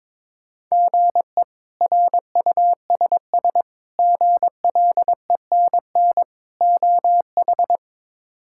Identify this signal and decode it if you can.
Morse code, heard as a keyed tone, about 22 wpm, 700 Hz. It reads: GE RUSS GLENN OH
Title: K1USN SST QSO Practice - 22wpm